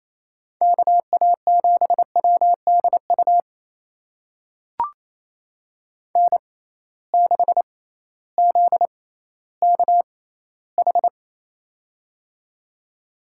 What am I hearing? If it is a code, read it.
Morse code, heard as a keyed tone, about 28 wpm, spaced as for 15 wpm, 700 Hz. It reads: KA7WDU N6ZKH
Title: USA Call Signs - 6x Character Spacing - 28wpm